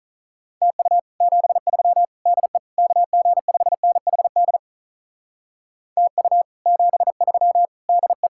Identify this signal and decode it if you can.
Morse code, heard as a keyed tone, about 46 wpm, 700 Hz. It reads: TU 73 DE KG5NHD TU 73 DE